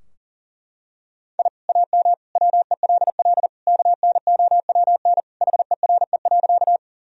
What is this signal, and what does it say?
Morse code, heard as a keyed tone, about 40 wpm, 700 Hz. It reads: I AM WELL KNOWN HERE.
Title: Sentences from Top 1000 Words - Review Part 3 - 40wpm